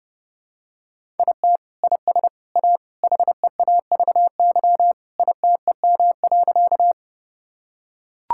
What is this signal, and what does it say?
Morse code, heard as a keyed tone, about 30 wpm, 700 Hz. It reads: IT IS A HEAVY ITEM.
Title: Sentences from Top 1000 Words - Review Part 6 - 30wpm